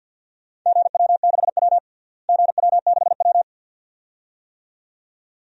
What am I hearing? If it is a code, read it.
Morse code, heard as a keyed tone, about 46 wpm, 700 Hz. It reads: GW6W GW6W
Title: DXCC-100 Call Signs - 4 Characters - Repeated 2x - 50wpm